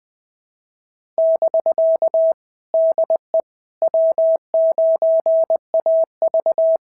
Morse code, heard as a keyed tone, about 20 wpm, 650 Hz. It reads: <BK> DE W9AV
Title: POTA QSO Practice - 20wpm